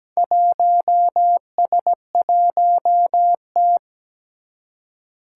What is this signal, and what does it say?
Morse code, heard as a keyed tone, about 17 wpm, 700 Hz. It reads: 1S1T
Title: Four Challenging Letter-Number Combinations 17wpm